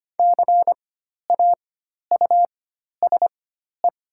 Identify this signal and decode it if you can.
Morse code, heard as a keyed tone, about 25 wpm, spaced as for 12 wpm, 700 Hz. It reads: CAUSE
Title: 4x Character Spacing - Top 200 Words - 25wpm